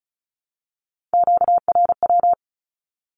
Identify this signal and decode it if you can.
Morse code, heard as a keyed tone, about 34 wpm, 700 Hz. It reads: QRW
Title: Q-Codes Repeated 3x 35wpm